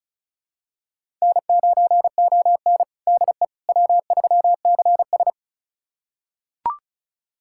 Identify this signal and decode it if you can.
Morse code, heard as a keyed tone, about 36 wpm, 700 Hz. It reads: N9ON DE W3CS
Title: USA Callsign DE Callsign - 4 Characters - 35wpm